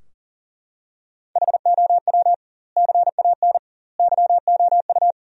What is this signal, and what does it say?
Morse code, heard as a keyed tone, about 40 wpm, 700 Hz. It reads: HOW CAN YOU
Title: 2x Word-Spacing - Sentences from Top 200 Words - 40wpm